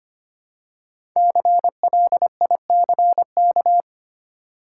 Morse code, heard as a keyed tone, about 25 wpm, 700 Hz. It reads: CLICK
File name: 5 Letter Words - First 3 Letters Given - 25wpm